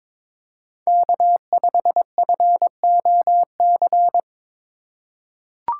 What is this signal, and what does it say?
Morse code, heard as a keyed tone, about 22 wpm, 700 Hz. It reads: K5FOC
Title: USA Call Signs - 5 Characters - 2x Character Spacing - 22wpm